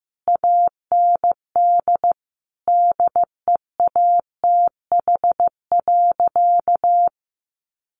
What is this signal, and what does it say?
Morse code, heard as a keyed tone, about 15 wpm, 700 Hz. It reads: AND DEATH.
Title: Sentences from Top 1100 Words 15wpm